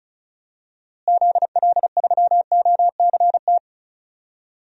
Morse code, heard as a keyed tone, about 34 wpm, 700 Hz. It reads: ZL3OCT
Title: DXCC-100 Call Signs - 6 Characters - Repeated 3x - 35wpm